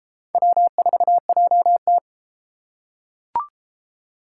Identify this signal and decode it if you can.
Morse code, heard as a keyed tone, about 29 wpm, 700 Hz. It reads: W4JT